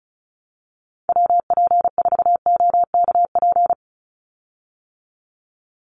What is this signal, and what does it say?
Morse code, heard as a keyed tone, about 34 wpm, 700 Hz. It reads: WP4OKP
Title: USA Call Signs 35wpm